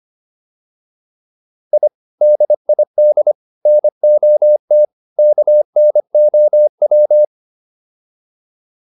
Morse code, heard as a keyed tone, about 25 wpm, 600 Hz. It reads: I DID NOT KNOW
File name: WD304 25